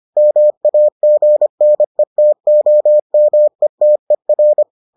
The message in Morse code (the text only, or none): MAGNETOMETER